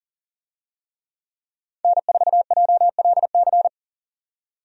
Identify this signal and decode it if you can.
Morse code, heard as a keyed tone, about 40 wpm, 700 Hz. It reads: N4JLC
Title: USA Call Signs - 5 Characters - 40wpm